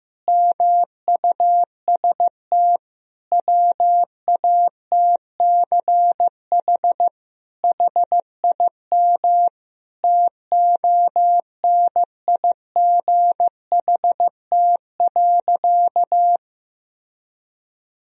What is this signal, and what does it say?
Morse code, heard as a keyed tone, about 15 wpm, 700 Hz. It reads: MUST WATCH HIM TONIGHT.
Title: Sentences from Top 1000 Words - Review Part 3 - 15wpm